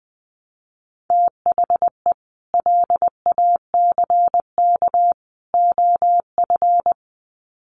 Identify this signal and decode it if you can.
Morse code, heard as a keyed tone, about 20 wpm, 700 Hz. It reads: THE LACK OF
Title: Sets of 3 Words 20wpm